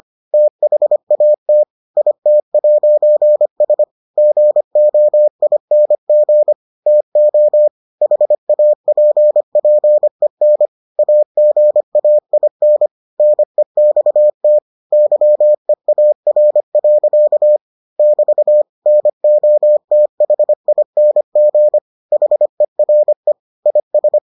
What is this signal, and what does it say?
Morse code, heard as a keyed tone, about 25 wpm, 600 Hz. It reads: THAT IT'S GOING TO HAPPEN AGAIN NEXT YEAR. <BT> NOTHING HERE IS